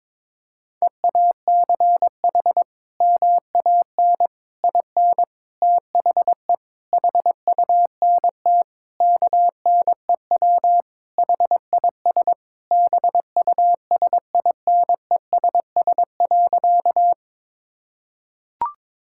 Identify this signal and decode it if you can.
Morse code, heard as a keyed tone, about 22 wpm, 700 Hz. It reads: EACH MAN IN THE HUNT KNEW HIS BUSINESS.
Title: Sentences from Top 1600 Words 22wpm